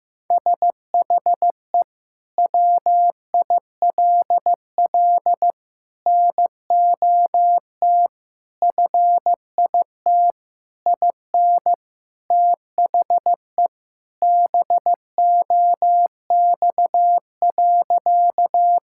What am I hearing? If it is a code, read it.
Morse code, heard as a keyed tone, about 15 wpm, 700 Hz. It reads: SHE WILL NOT FIT IN THE BOX.
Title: Sentences from Top 1000 Words - Review Part 5 - 15wpm